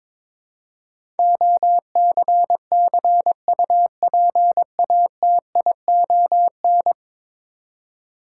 Morse code, heard as a keyed tone, about 23 wpm, 700 Hz. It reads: OCCUPATION
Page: morse-code-ninja-practice-22wpm